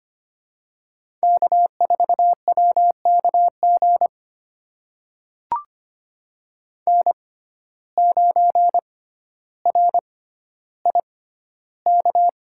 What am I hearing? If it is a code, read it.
Morse code, heard as a keyed tone, about 25 wpm, spaced as for 13 wpm, 700 Hz. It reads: K4WKG N9RIK
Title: USA Call Signs - 5 Characters - 6x Character Spacing - 25wpm